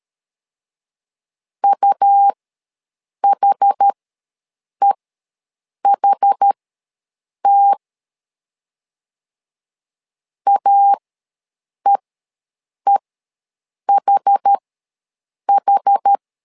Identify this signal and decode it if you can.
Morse code, heard as a keyed tone, about 13 wpm, spaced as for 7 wpm, 800 Hz. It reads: UHEHT AEEHH
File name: LEZIONI / 2 UHDN / 2 UH ws 12wpm 1730